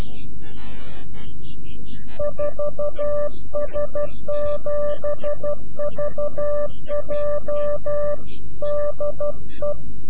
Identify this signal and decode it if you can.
Morse code, heard as a keyed tone, about 12 wpm, 590 Hz. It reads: 4S7VJ DE
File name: CW-QSO / CQ Call / CQ-QSO